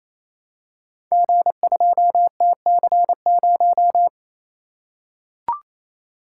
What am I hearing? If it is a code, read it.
Morse code, heard as a keyed tone, about 29 wpm, 700 Hz. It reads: G2TC0
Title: Code Groups - 5 Letter:Numbers - 28wpm